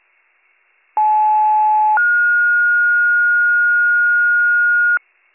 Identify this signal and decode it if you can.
Two-tone page, heard as a one-second tone, then a three-second tone, about 860 then 1430 Hz, with noise. It is heard from 1.0 to 5.0 s.